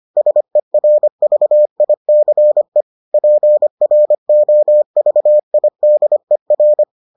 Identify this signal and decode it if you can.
Morse code, heard as a keyed tone, about 25 wpm, 600 Hz. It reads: SERVICE PROVIDER